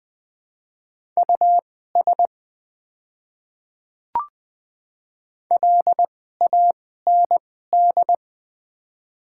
Morse code, heard as a keyed tone, about 20 wpm, spaced as for 14 wpm, 700 Hz. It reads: US LAND
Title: 2x Character Spacing - Top 300 Words - 20wpm